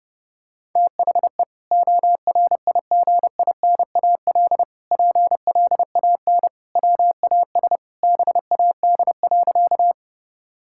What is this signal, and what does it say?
Morse code, heard as a keyed tone, about 30 wpm, 700 Hz. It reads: THE ORIGINAL PLAN WAS BAD.
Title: Sentences from Top 1000 Words - Review Part 6 - 30wpm